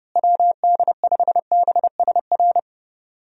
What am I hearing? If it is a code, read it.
Morse code, heard as a keyed tone, about 30 wpm, 700 Hz. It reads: WD5BSR
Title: USA Call Signs - 6 Characters - Repeated 3x - 30wpm